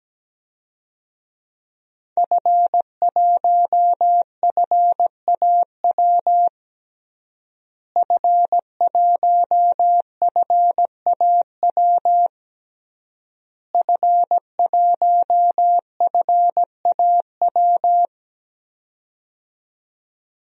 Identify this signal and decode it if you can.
Morse code, heard as a keyed tone, about 17 wpm, 700 Hz. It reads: F1FAW F1FAW F1FAW
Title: DXCC-100 Call Signs - 5 Characters - Repeated 3x - 17wpm